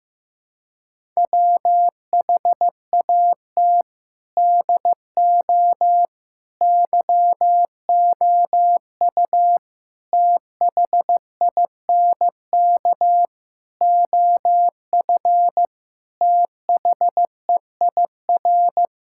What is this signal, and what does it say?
Morse code, heard as a keyed tone, about 16 wpm, 700 Hz. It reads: WHAT DO YOU THINK OF THEIR
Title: Sentences from Top 1000 Words - Review Part 5 - 15wpm